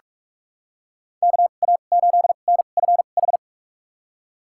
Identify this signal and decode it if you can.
Morse code, heard as a keyed tone, about 46 wpm, spaced as for 32 wpm, 700 Hz. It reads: KA8NFH
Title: USA Call Signs - 6 Characters - 2x Character Spacing - 45wpm